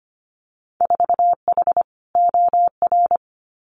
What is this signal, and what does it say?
Morse code, heard as a keyed tone, about 25 wpm, 700 Hz. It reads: 4H OR
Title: Winter Field Day Exchanges Encore 25wpm